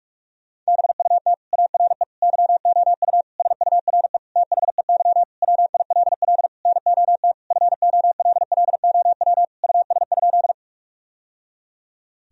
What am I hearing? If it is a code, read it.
Morse code, heard as a keyed tone, about 46 wpm, 700 Hz. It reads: BUT ARE YOU SURE THEY WILL NOT FOLLOW US?
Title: Sentences from Top 1000 Words - Review Part 1 - 45wpm